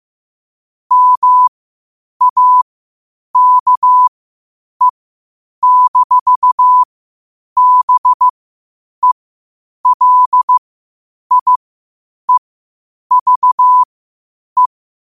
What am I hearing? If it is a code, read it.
Morse code, heard as a keyed tone, about 15 wpm, spaced as for 8 wpm, 1000 Hz. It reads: MAKE-BELIEVE
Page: Quotes for Fri, 15 Aug 2025 in Morse Code at 8 words per minute.